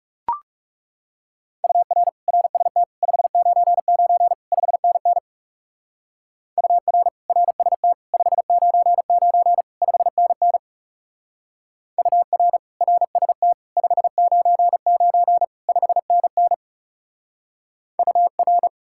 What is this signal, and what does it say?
Morse code, heard as a keyed tone, about 35 wpm, 700 Hz. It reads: UR RST 599 5NN UR RST 599 5NN UR RST 599 5NN UR
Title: Speed-Racing Common QSO Phrases - 30wpm